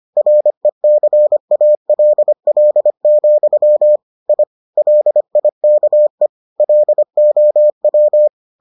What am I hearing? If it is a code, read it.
Morse code, heard as a keyed tone, about 25 wpm, 600 Hz. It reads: RECALL, I LIKE LOW